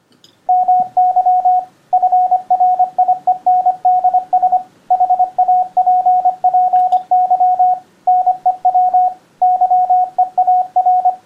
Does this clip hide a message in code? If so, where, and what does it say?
Yes: Morse code from 0.5 to 11.1 s, heard as a keyed tone, about 25 wpm, 730 Hz, with radio noise. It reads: MY FRIENDS HAPPY NEW YEAR